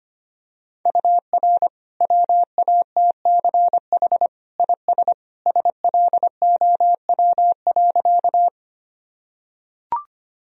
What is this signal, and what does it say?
Morse code, heard as a keyed tone, about 25 wpm, 700 Hz. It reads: UR WATCH IS SLOW.
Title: Sentences from Top 400 Words and Abbreviations 25wpm